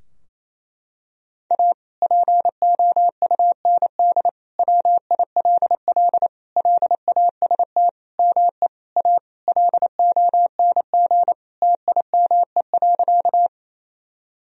Morse code, heard as a keyed tone, about 28 wpm, 700 Hz. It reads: A POUND WILL LAST ME A LONG TIME.
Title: Sentences from Top 1000 Words - Review Part 2 - 28wpm